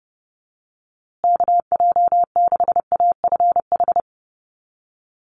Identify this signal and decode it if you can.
Morse code, heard as a keyed tone, about 30 wpm, 700 Hz. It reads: KJ6AFH
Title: USA Call Signs Repeated 3x 30wpm